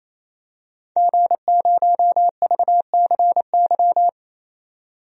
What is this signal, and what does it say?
Morse code, heard as a keyed tone, about 28 wpm, 700 Hz. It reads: G0VCY
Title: DXCC-100 Call Signs - 5 Characters - 28wpm